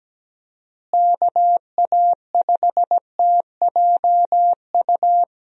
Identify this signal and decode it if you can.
Morse code, heard as a keyed tone, about 17 wpm, 700 Hz. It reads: KA5TJU
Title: USA Call Signs - 6 Characters - Repeated 2x - 17wpm